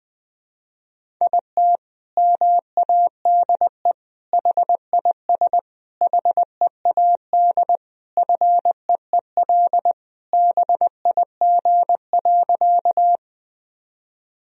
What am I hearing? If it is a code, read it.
Morse code, heard as a keyed tone, about 21 wpm, 700 Hz. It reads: IT MADE HIS HEAD FEEL BIG.
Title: Sentences from Top 1000 Words - Review Part 1 - 20wpm